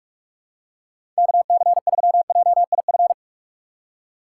Morse code, heard as a keyed tone, about 43 wpm, 700 Hz. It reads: KC3JIF
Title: USA Call Signs - 6 Characters - 6x Character Spacing - 45wpm